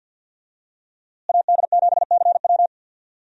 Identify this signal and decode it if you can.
Morse code, heard as a keyed tone, about 48 wpm, 700 Hz. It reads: AD7CW
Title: USA Call Signs - 5 Characters - Repeated 3x - 50wpm